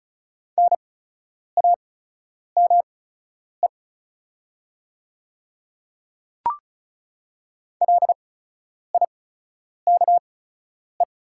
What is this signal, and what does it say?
Morse code, heard as a keyed tone, about 34 wpm, spaced as for 10 wpm, 700 Hz. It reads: NAME LIKE
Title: 8x Character Spacing - Top 300 Words - 35wpm